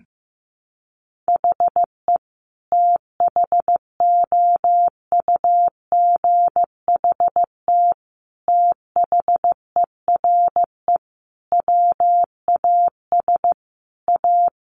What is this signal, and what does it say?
Morse code, heard as a keyed tone, about 15 wpm, 700 Hz. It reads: HE THOUGHT THERE WAS A
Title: Sentences from Top 1300 Words 15wpm